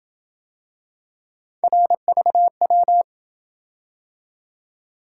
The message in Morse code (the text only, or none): RVW